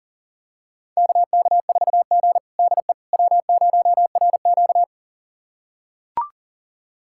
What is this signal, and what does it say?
Morse code, heard as a keyed tone, about 40 wpm, 700 Hz. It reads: KK4G DE W0RQ
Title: USA Callsign DE Callsign - 4 Characters - 40wpm